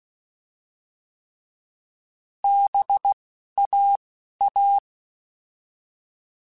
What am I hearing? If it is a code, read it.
Morse code, heard as a keyed tone, about 16 wpm, spaced as for 11 wpm, 800 Hz. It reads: BAA
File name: Alphabetical / Alphabetical mp3s / Bs @10